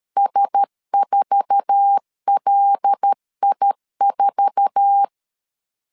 Morse code, heard as a keyed tone, about 13 wpm, 800 Hz. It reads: S4LI4
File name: LEZIONI / 8 YJ45 / 8 YJ4 12wpm 2151